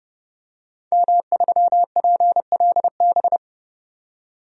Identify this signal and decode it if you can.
Morse code, heard as a keyed tone, about 30 wpm, 700 Hz. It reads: M3PLB
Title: DXCC-100 Call Signs - 5 Characters - 30wpm